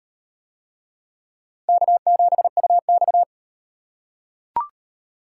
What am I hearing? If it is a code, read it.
Morse code, heard as a keyed tone, about 37 wpm, 700 Hz. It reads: K7UX